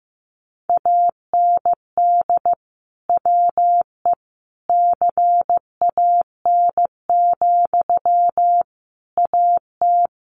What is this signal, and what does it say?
Morse code, heard as a keyed tone, about 15 wpm, 700 Hz. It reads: AND WE CAN, AT
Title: Sentences from Top 900 Words 15wpm